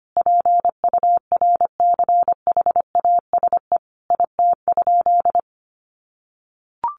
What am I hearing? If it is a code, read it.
Morse code, heard as a keyed tone, about 25 wpm, 700 Hz. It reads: PURCHASE IT?
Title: Sentences from Top 800 Words 25wpm